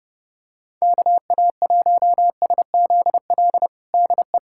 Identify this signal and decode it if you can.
Morse code, heard as a keyed tone, about 30 wpm, 700 Hz. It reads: KA1SZL DE
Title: USA Callsign DE Callsign - 6 Characters - 30wpm